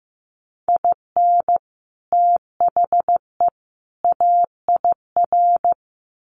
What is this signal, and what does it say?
Morse code, heard as a keyed tone, about 15 wpm, 700 Hz. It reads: IN THE AIR
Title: Sets of 3 Words 15wpm